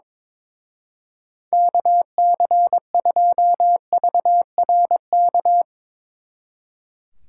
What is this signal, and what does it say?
Morse code, heard as a keyed tone, about 22 wpm, 700 Hz. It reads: KC2VRK
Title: Speed Racing - USA 2x3 Call Signs 22wpm